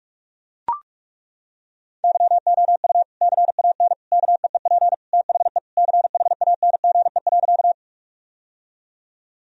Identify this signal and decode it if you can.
Morse code, heard as a keyed tone, about 41 wpm, 700 Hz. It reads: YOU CAN KEEP THE CHANGE.
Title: Sentences from Top 1000 Words - Review Part 1 - 45wpm